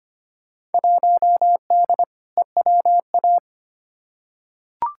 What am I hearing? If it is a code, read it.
Morse code, heard as a keyed tone, about 25 wpm, 700 Hz. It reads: 1D EWA
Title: Field Day Exchanges - 25wpm